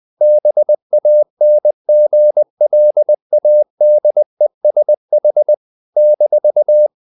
Morse code, heard as a keyed tone, about 20 wpm, 600 Hz. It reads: BANGLADESH -
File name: CW / Practice / NationalCapitals20